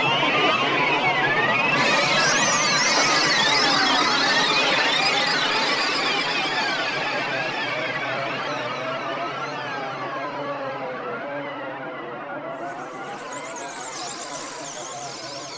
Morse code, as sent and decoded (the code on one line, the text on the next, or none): none